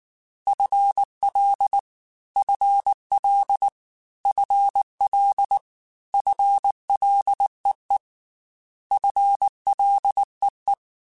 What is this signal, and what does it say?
Morse code, heard as a keyed tone, about 19 wpm, 800 Hz. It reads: FL FL FL FLEE FLEE
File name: FL / FL mp3s / FL Prefixes 3x2 @18